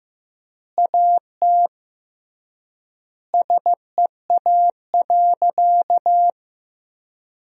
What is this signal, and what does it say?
Morse code, heard as a keyed tone, about 15 wpm, 700 Hz. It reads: AT SEA.
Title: 3x Word-Spacing - Sentences from Top 300 Words - 15wpm